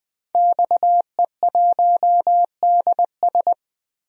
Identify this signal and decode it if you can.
Morse code, heard as a keyed tone, about 21 wpm, 700 Hz. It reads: XE1DS
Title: DXCC-100 Call Signs - 5 Characters - 20wpm